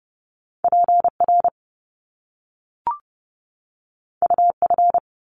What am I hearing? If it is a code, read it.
Morse code, heard as a keyed tone, about 30 wpm, 700 Hz. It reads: PR UF
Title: Top Two Letter Combinations in Words 30wpm